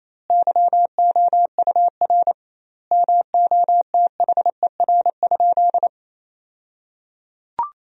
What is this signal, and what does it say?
Morse code, heard as a keyed tone, about 28 wpm, 700 Hz. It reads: YOUR MOTHER?
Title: 2x Word-Spacing - Sentences from Top 200 Words - 28wpm